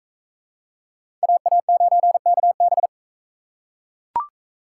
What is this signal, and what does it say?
Morse code, heard as a keyed tone, about 39 wpm, 700 Hz. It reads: AA9KB